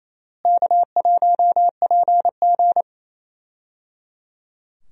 Morse code, heard as a keyed tone, about 28 wpm, 700 Hz. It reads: K1PG
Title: USA Call Signs - 4 Characters - Repeated 3x - 28wpm